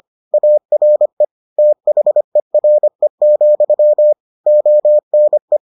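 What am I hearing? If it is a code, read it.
Morse code, heard as a keyed tone, about 25 wpm, 600 Hz. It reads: ARE THERE, ONE
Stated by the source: Morse edition of the Foundations of Amateur Radio episodes.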